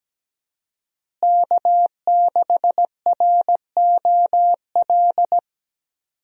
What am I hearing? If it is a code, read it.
Morse code, heard as a keyed tone, about 17 wpm, 700 Hz. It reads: K6ROL
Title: USA Call Signs - 5 Characters - Repeated 2x - 17wpm